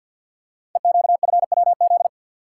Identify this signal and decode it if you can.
Morse code, heard as a keyed tone, about 48 wpm, 700 Hz. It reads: EQFWZ
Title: Code Groups - 5 Letter:Numbers - 50wpm